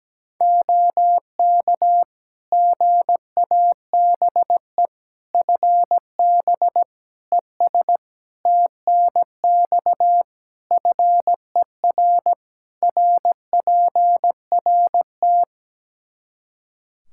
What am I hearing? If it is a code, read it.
Morse code, heard as a keyed tone, about 17 wpm, 700 Hz. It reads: OK GABE FB ES TNX FER RPRT
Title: Practice QSO - Exchange 2 - Info - 17wpm